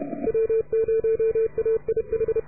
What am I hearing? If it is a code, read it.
Morse code, heard as a keyed tone, about 32 wpm, 440 Hz. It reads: W0AIH